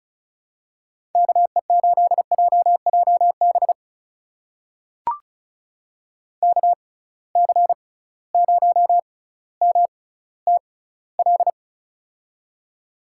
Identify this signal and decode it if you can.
Morse code, heard as a keyed tone, about 36 wpm, spaced as for 18 wpm, 700 Hz. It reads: KE8JJB KC0MTL
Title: USA Call Signs - 6 Characters - 6x Character Spacing - 35wpm